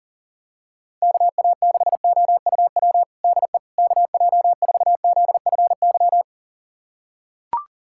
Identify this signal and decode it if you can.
Morse code, heard as a keyed tone, about 40 wpm, 700 Hz. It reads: KA6OUW DE KJ4ZFY
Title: USA Callsign DE Callsign - 6 Characters - 40wpm